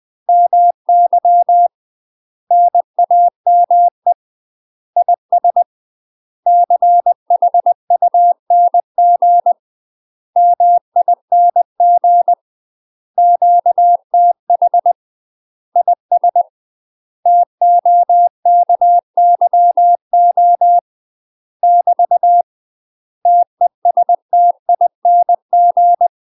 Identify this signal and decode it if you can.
Morse code, heard as a keyed tone, about 20 wpm, 700 Hz. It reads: MY NAME IS CHUNG MING QTH IS TOKYO <BT> TESTING
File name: UK MASTER QSOs TRAINING / 20WPM / UK MASTER QSOs 0044